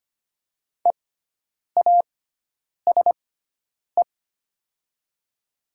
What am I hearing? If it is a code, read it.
Morse code, heard as a keyed tone, about 25 wpm, spaced as for 9 wpm, 700 Hz. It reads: EASE
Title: 6x Character Spacing - Top 300 Words - 25wpm